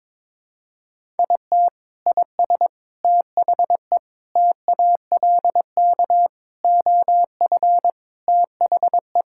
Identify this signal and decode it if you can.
Morse code, heard as a keyed tone, about 22 wpm, 700 Hz. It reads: IT IS THE TALK OF THE
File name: practice / 22wpm / Sentences with 4-Word Phrases - 22wpm